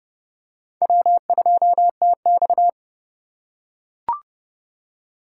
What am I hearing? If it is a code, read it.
Morse code, heard as a keyed tone, about 30 wpm, 700 Hz. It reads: W2TX
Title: USA Call Signs - 4 Characters - 3x Character Spacing - 30wpm